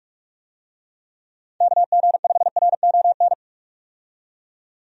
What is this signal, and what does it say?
Morse code, heard as a keyed tone, about 46 wpm, 700 Hz. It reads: KG5RON
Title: USA Call Signs - 6 Characters - 45wpm